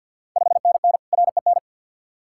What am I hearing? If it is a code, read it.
Morse code, heard as a keyed tone, about 51 wpm, 690 Hz. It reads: 5NN REN
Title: Ontario QSO Party Exchange 50wpm